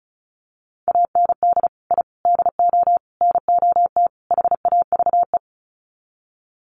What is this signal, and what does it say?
Morse code, heard as a keyed tone, about 34 wpm, 700 Hz. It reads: AND I DO NOT HAVE
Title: Sets of 5 Words Encore 35wpm